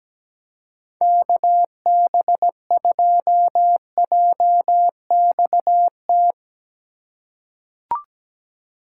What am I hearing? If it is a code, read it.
Morse code, heard as a keyed tone, about 17 wpm, 700 Hz. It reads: KB2JXT
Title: USA Call Signs - 2x Character Spacing - 17wpm